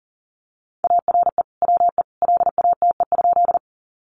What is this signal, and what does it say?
Morse code, heard as a keyed tone, about 37 wpm, 700 Hz. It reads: ARE WE LATE?
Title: Sentences from Top 300 Words 40wpm